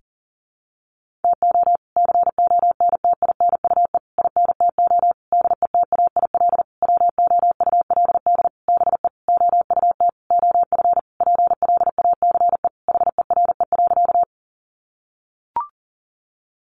Morse code, heard as a keyed tone, about 40 wpm, 700 Hz. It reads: TO CONTINUE INTO DETAIL WOULD BE OUT OF PLACE HERE.
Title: Sentences from Top 1500 Words 40wpm